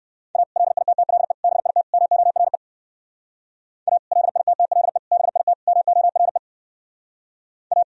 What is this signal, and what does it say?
Morse code, heard as a keyed tone, about 59 wpm, 700 Hz. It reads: A LITTLE BIT MORE A LITTLE BIT MORE A